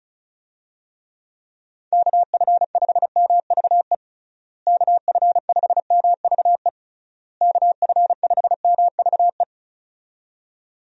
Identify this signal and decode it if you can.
Morse code, heard as a keyed tone, about 34 wpm, 700 Hz. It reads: KF5MVE KF5MVE KF5MVE
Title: USA Call Signs - 6 Characters - Repeated 3x - 35wpm